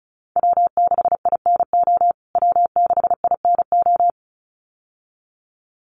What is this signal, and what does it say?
Morse code, heard as a keyed tone, about 34 wpm, 700 Hz. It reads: W6INO W6INO
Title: USA Call Signs Repeated 2x 35wpm